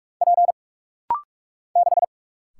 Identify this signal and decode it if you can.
Morse code, heard as a keyed tone, about 44 wpm, 700 Hz. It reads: P 6
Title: Single Letter-Number - Rapid-Fire 45wpm